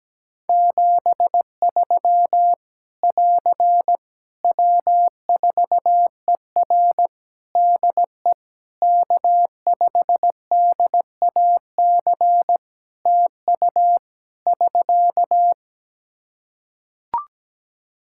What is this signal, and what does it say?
Morse code, heard as a keyed tone, about 17 wpm, 700 Hz. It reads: 73 <AR> W4ER DE K5DAC TU <SK>
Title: K1USN SST QSO Practice 17wpm